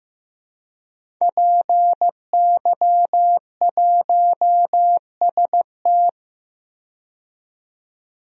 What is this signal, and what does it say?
Morse code, heard as a keyed tone, about 15 wpm, 700 Hz. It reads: PY1ST
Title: DXCC-100 Call Signs - 5 Characters - 15wpm